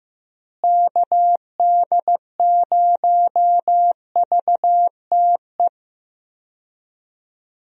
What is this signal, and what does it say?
Morse code, heard as a keyed tone, about 15 wpm, 700 Hz. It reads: KD0VTE
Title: USA Call Signs - 6 Characters - Repeated 3x - 15wpm